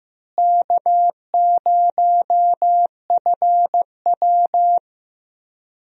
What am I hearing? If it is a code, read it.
Morse code, heard as a keyed tone, about 15 wpm, 700 Hz. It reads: K0FW
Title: USA Call Signs - 4 Characters - Repeated 2x - 15wpm